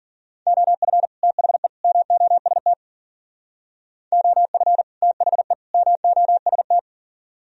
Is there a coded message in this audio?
Yes: Morse code from 0.5 to 6.8 s, heard as a keyed tone, about 51 wpm, 700 Hz. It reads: OF THE MOST OF THE MOST